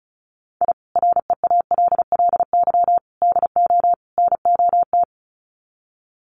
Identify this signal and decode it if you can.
Morse code, heard as a keyed tone, about 34 wpm, 700 Hz. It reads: I REALLY DO NOT
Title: Sets of 4 Words 35wpm